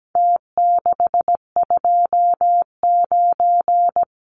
Morse code, heard as a keyed tone, about 17 wpm, 700 Hz. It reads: T629
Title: Numbers - Contest Style - 4-Digit 17wpm